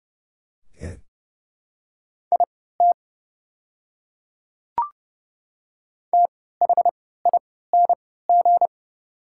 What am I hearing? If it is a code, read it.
Morse code, heard as a keyed tone, about 30 wpm, spaced as for 17 wpm, 700 Hz. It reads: IT THING
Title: 3x Character Spacing - Top 100 Words - 30wpm